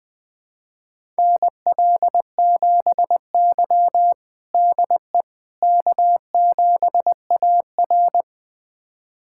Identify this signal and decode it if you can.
Morse code, heard as a keyed tone, about 20 wpm, 700 Hz. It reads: NL7Y DE K7AR
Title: USA Callsign DE Callsign - 4 Characters - 20wpm